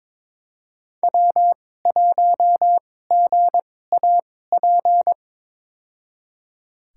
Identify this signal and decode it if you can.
Morse code, heard as a keyed tone, about 22 wpm, spaced as for 16 wpm, 700 Hz. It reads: W1GAP
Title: USA Call Signs - 2x Character Spacing - 22wpm